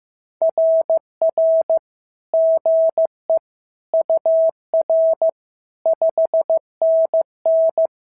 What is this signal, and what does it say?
Morse code, heard as a keyed tone, about 15 wpm, 690 Hz. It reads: RR GE UR 5NN
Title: SOTA QSO Practice 15wpm